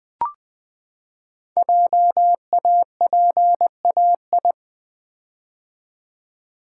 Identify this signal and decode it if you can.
Morse code, heard as a keyed tone, about 20 wpm, 700 Hz. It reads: JAPAI
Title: Code Groups - 5 Letter:Numbers - 20wpm